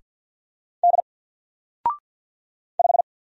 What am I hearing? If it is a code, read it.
Morse code, heard as a keyed tone, about 50 wpm, 700 Hz. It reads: D 5
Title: Single Letter-Number 50wpm